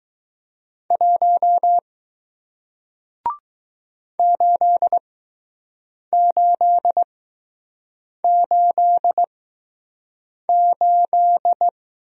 Morse code, heard as a keyed tone, about 20 wpm, spaced as for 6 wpm, 700 Hz. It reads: 1 8888